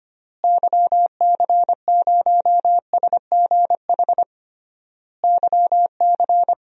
Morse code, heard as a keyed tone, about 25 wpm, 700 Hz. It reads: YC0SGH YC
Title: DXCC-100 Call Signs - 6 Characters - Repeated 3x - 25wpm